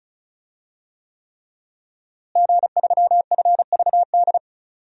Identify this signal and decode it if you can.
Morse code, heard as a keyed tone, about 36 wpm, 700 Hz. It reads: G3FVD
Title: DXCC-100 Call Signs - 5 Characters - Repeated 3x - 35wpm